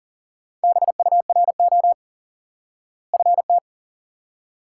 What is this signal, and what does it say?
Morse code, heard as a keyed tone, about 40 wpm, 700 Hz. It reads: BURO FT
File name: IQE Course 3 - Lesson 89 - YAGI - Two Elements - 40wpm